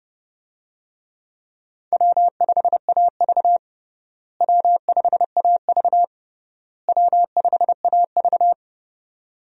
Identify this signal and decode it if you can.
Morse code, heard as a keyed tone, about 30 wpm, 700 Hz. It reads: W5AV W5AV W5AV
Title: USA Call Signs - 4 Characters - Repeated 3x - 30wpm